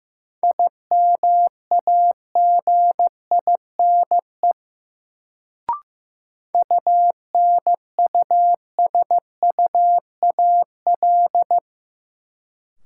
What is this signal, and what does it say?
Morse code, heard as a keyed tone, about 15 wpm, 700 Hz. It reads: IMAGINE UNUSUAL
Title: Sentences from Top 1400 Words - New Word Drill Alt - 15wpm